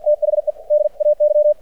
Morse code, heard as a keyed tone, about 53 wpm, spaced as for 47 wpm, 600 Hz. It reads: THE DAY
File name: archived / CRSnet